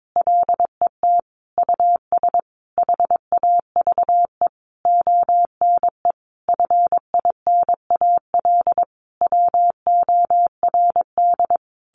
Morse code, heard as a keyed tone, about 22 wpm, 700 Hz. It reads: LET US HAVE ONE FINAL WORD
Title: Sentences from Top 1000 Words 22wpm